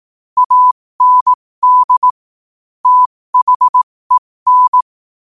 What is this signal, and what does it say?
Morse code, heard as a keyed tone, about 18 wpm, spaced as for 15 wpm, 1000 Hz. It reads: AND THEN
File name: qotdfiles / 15wpm